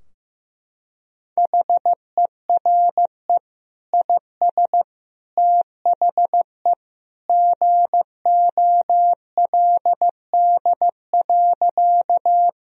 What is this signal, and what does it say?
Morse code, heard as a keyed tone, about 15 wpm, 700 Hz. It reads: HERE IS THE GOLD.
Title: Sentences from Top 1000 Words - Review Part 3 - 15wpm